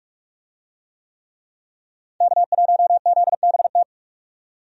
Morse code, heard as a keyed tone, about 46 wpm, 700 Hz. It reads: K1ZBT
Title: USA Call Signs - 5 Characters - 45wpm